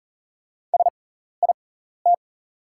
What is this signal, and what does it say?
Morse code, heard as a keyed tone, about 40 wpm, spaced as for 14 wpm, 690 Hz. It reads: SIT
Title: 6x Character Spacing - Top 500 Words - 40wpm